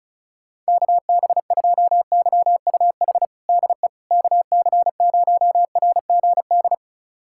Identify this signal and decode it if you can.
Morse code, heard as a keyed tone, about 34 wpm, 700 Hz. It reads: KB2YUH DE KC0RGD
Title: USA Callsign DE Callsign - 6 Characters - 35wpm